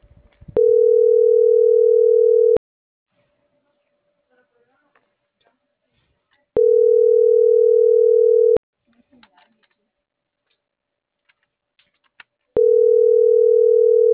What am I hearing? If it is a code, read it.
Ringback tone (North America), 440 + 480 Hz.